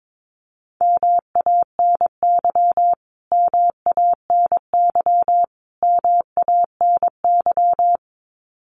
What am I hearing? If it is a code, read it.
Morse code, heard as a keyed tone, about 22 wpm, 700 Hz. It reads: MANY MANY MANY
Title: Top 500 Words Repeated 22wpm